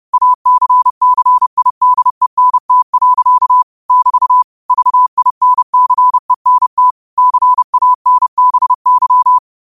Morse code, heard as a keyed tone, about 30 wpm, 1000 Hz. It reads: ACCIDENT. <BT> VINCENT CANBY
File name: qotdfiles / 30wpm